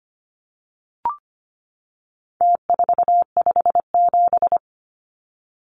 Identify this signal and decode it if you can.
Morse code, heard as a keyed tone, about 25 wpm, 700 Hz. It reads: T457
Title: Numbers - Contest Style - 4-Digit 25wpm